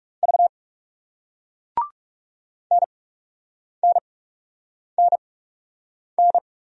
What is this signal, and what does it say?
Morse code, heard as a keyed tone, about 41 wpm, spaced as for 8 wpm, 700 Hz. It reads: V NNNN